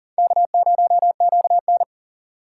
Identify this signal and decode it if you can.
Morse code, heard as a keyed tone, about 37 wpm, 700 Hz. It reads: K0QN
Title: USA Call Signs - 4 Characters - 4x Character Spacing - 40wpm